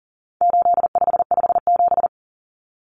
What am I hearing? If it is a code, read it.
Morse code, heard as a keyed tone, about 40 wpm, 700 Hz. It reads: 8557
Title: Numbers - 4-Digit 40wpm